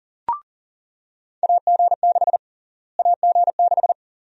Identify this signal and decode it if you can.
Morse code, heard as a keyed tone, about 40 wpm, 700 Hz. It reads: AG6 AG6
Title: USA Call Sign Prefixes - Repeated 2x - 40wpm